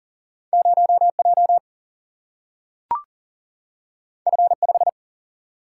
Morse code, heard as a keyed tone, about 40 wpm, 700 Hz. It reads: 0J F5
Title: Single-Letter Single-Number 40wpm